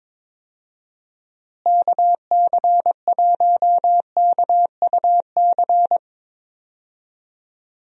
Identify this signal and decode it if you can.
Morse code, heard as a keyed tone, about 22 wpm, 700 Hz. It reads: KC1KUC
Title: USA Call Signs - 6 Characters - Repeated 3x - 22wpm